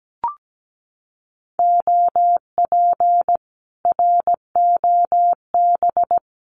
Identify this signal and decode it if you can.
Morse code, heard as a keyed tone, about 17 wpm, 700 Hz. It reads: OP ROB
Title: Common QSO Phrases Encore 17wpm